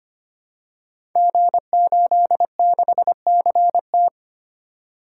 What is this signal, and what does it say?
Morse code, heard as a keyed tone, about 25 wpm, 700 Hz. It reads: G86CT
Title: Code Groups - 5 Letter:Numbers - 25wpm